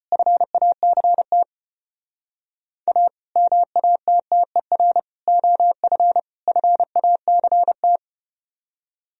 Morse code, heard as a keyed tone, about 34 wpm, 700 Hz. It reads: FACT A MATTER OF FACT